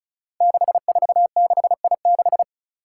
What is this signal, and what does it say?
Morse code, heard as a keyed tone, about 36 wpm, 700 Hz. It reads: 646I6
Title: Five Challenging Letter-Number Combinations 35wpm